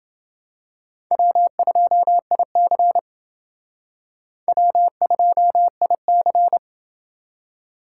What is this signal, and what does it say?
Morse code, heard as a keyed tone, about 29 wpm, 700 Hz. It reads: W2IC W2IC